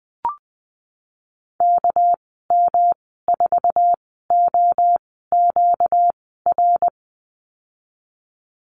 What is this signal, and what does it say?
Morse code, heard as a keyed tone, about 20 wpm, 700 Hz. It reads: K M 4 O Q R
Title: 238 Lesson 38g - Letter q - Call Signs 20wpm